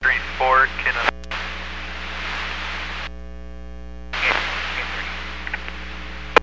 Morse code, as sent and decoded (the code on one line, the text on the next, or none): none